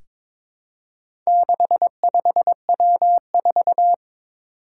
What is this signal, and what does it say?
Morse code, heard as a keyed tone, about 22 wpm, 700 Hz. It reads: 65W4
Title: Four Challenging Letter-Number Combinations 22wpm